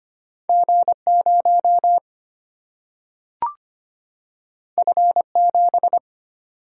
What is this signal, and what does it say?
Morse code, heard as a keyed tone, about 25 wpm, 700 Hz. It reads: G0 F7
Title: Single-Letter Single-Number 25wpm